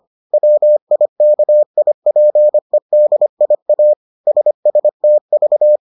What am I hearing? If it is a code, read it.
Morse code, heard as a keyed tone, about 25 wpm, 600 Hz. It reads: WIKIPEDIA SSTV